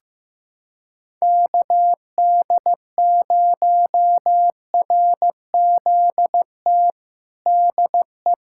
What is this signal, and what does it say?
Morse code, heard as a keyed tone, about 16 wpm, 700 Hz. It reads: KD0RZT DE
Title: USA Callsign DE Callsign 15wpm